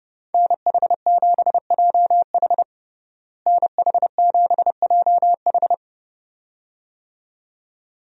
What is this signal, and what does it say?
Morse code, heard as a keyed tone, about 30 wpm, 700 Hz. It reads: NH7JH NH7JH
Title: USA Call Signs - 5 Characters - Repeated 2x - 30wpm